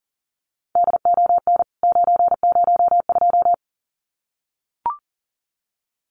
Morse code, heard as a keyed tone, about 40 wpm, 700 Hz. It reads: DON 902
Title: CWT Exchange 40wpm